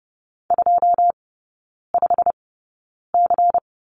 Morse code, heard as a keyed tone, about 30 wpm, spaced as for 9 wpm, 690 Hz. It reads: 25C
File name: Lesson 16e - Letter c - Three Characters 30wpm